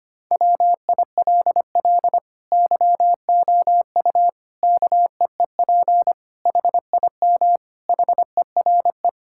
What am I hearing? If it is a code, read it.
Morse code, heard as a keyed tone, about 25 wpm, 700 Hz. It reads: WILL YOU KEEP HIM HERE
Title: Sentences from Top 1600 Words 25wpm